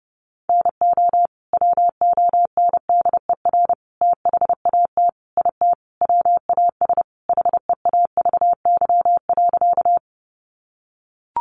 Morse code, heard as a keyed tone, about 28 wpm, 700 Hz. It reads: NO WONDER THAT IT WAS HEAVY.
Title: Sentences from Top 1100 Words 30wpm